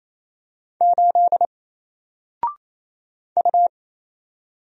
Morse code, heard as a keyed tone, about 27 wpm, 700 Hz. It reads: 8 U
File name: Single-Letter-Number-28wpm